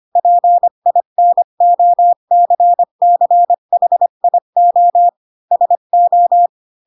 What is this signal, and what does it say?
Morse code, heard as a keyed tone, about 25 wpm, 700 Hz. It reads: PINOCCHIO SO
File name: Pinocchio / Pinocchio ch8 20 25 ik1hge